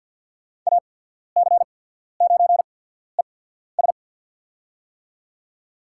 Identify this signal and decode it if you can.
Morse code, heard as a keyed tone, about 52 wpm, spaced as for 14 wpm, 700 Hz. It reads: AC9ES
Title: USA Call Signs - 5 Characters - 8x Character Spacing - 50wpm